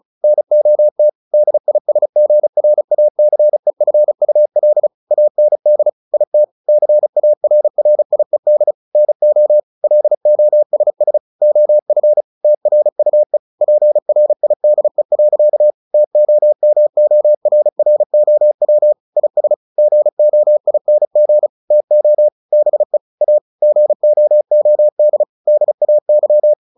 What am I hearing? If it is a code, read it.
Morse code, heard as a keyed tone, about 34 wpm, 600 Hz. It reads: NOT DISGRACEFUL AND IT CARRIED NO LOSS OF TRUE PRIDE. TOMORROW IS GOING TO BE A GOOD DAY
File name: the old man and the sea / the old man and the sea 1 35wpm